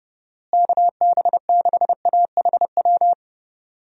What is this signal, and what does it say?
Morse code, heard as a keyed tone, about 32 wpm, 700 Hz. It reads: KB6AHW
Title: USA Call Signs - 6 Characters - Repeated 3x - 30wpm